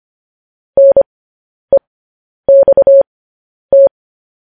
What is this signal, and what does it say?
Morse code, heard as a keyed tone, about 25 wpm, spaced as for 10 wpm, 550 Hz. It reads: NEXT
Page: news-headlines-in-morse-code-at-10-wpm